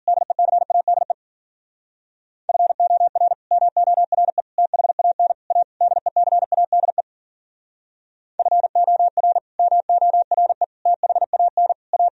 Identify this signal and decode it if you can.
Morse code, heard as a keyed tone, about 53 wpm, 700 Hz. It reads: DECADE FOR MORE THAN A DECADE FOR MORE THAN A